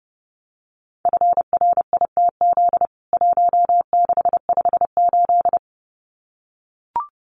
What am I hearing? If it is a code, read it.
Morse code, heard as a keyed tone, about 30 wpm, 700 Hz. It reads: FRITZ 1658
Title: CWT Exchange 30wpm